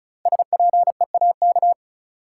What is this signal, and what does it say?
Morse code, heard as a keyed tone, about 36 wpm, 700 Hz. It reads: SPEAK
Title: Sentences from Top 1000 Words - Review Part 2 - New Word Drill - 35wpm